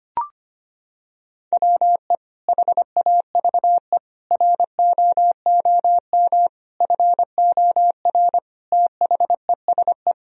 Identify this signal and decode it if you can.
Morse code, heard as a keyed tone, about 25 wpm, 700 Hz. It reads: WE HAVE ROOM FOR THESE
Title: Sentences from Top 1000 Words - Review Part 1 - 25wpm